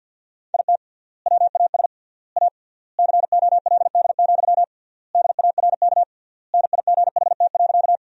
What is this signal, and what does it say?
Morse code, heard as a keyed tone, about 50 wpm, 700 Hz. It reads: IT WAS A COLD, DARK NIGHT.
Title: 3x Word-Spacing - Sentences from Top 500 Words - 50wpm